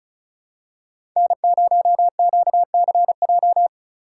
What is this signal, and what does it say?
Morse code, heard as a keyed tone, about 36 wpm, 700 Hz. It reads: N0QCJ
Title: USA Call Signs - 5 Characters - 4x Character Spacing - 35wpm